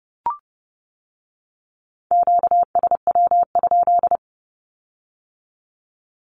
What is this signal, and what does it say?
Morse code, heard as a keyed tone, about 30 wpm, 700 Hz. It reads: QSW?
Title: Q-Codes with Spoken Meaning - 30wpm